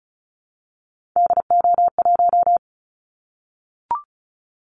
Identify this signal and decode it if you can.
Morse code, heard as a keyed tone, about 36 wpm, 700 Hz. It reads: DO1
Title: Callsign Prefixes for Top 25 DXCC - 35wpm